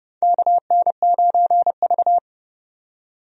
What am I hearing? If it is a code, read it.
Morse code, heard as a keyed tone, about 30 wpm, 700 Hz. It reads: KN9V
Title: USA Call Signs - 4 Characters - 30wpm